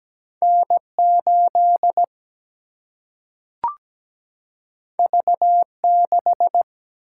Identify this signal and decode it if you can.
Morse code, heard as a keyed tone, about 17 wpm, 700 Hz. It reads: N8 V6
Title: Single-Letter Single-Number 17wpm